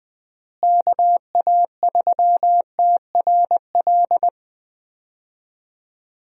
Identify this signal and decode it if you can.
Morse code, heard as a keyed tone, about 21 wpm, 700 Hz. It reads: KA3TRL
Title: USA Call Signs - 6 Characters - Repeated 3x - 20wpm